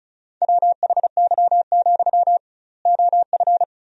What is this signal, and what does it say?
Morse code, heard as a keyed tone, about 34 wpm, 700 Hz. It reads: WHY, OF
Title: 2x Word-Spacing - Sentences from Top 500 Words - 35wpm